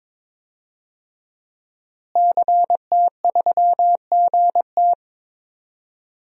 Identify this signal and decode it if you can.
Morse code, heard as a keyed tone, about 22 wpm, 700 Hz. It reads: CT3GT
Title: DXCC-100 Call Signs - 5 Characters - Repeated 2x - 22wpm